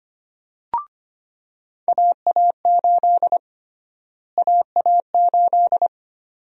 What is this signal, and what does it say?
Morse code, heard as a keyed tone, about 25 wpm, 700 Hz. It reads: AA8 AA8
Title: USA Call Sign Prefixes - Repeated 2x - 25wpm